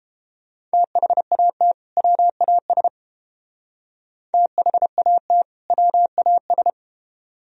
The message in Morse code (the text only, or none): THAT WAS THAT WAS